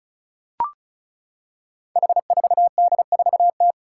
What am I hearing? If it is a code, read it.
Morse code, heard as a keyed tone, about 36 wpm, 700 Hz. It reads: H4D4T
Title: Five Challenging Letter-Number Combinations 35wpm